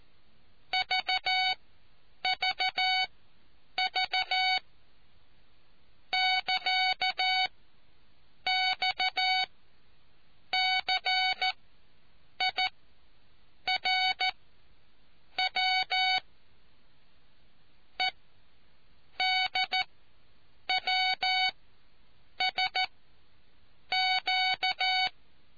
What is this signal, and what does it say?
Morse code, heard as a keyed tone, about 14 wpm, spaced as for 7 wpm, 750 Hz. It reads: VVV <KA> XCIRW EDWSQ